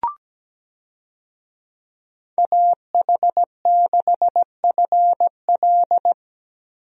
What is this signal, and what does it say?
Morse code, heard as a keyed tone, about 17 wpm, 700 Hz. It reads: AH6FL
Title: USA Call Signs - 5 Characters - Repeated 3x - 17wpm